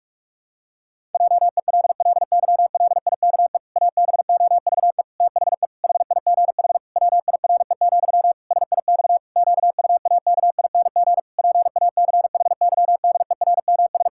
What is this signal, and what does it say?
Morse code, heard as a keyed tone, about 46 wpm, 700 Hz. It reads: JELLYLIKE ABOVE THE HIGH WIRE, SIX QUAKING PACHYDERMS
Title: Pangram Challenge v2 - 45wpm